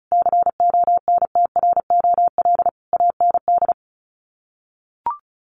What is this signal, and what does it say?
Morse code, heard as a keyed tone, about 34 wpm, 700 Hz. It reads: CONTROL AND
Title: Sets of 5 Words Encore2 35wpm